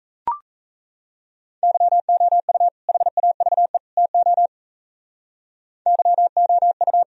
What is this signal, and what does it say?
Morse code, heard as a keyed tone, about 36 wpm, 700 Hz. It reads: YOU HAVE TO YOU